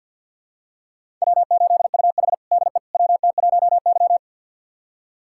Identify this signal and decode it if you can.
Morse code, heard as a keyed tone, about 50 wpm, 700 Hz. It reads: W8UH DE WT1Y
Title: USA Callsign DE Callsign - 4 Characters - 50wpm